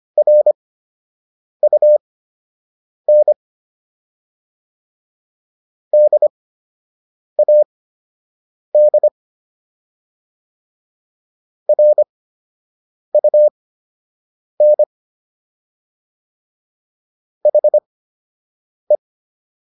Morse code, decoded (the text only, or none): RUN DAD RUN HE